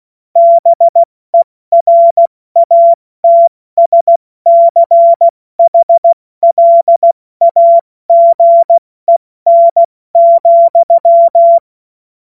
Morse code, heard as a keyed tone, about 16 wpm, spaced as for 14 wpm, 680 Hz. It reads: BERATSCHLAGEN,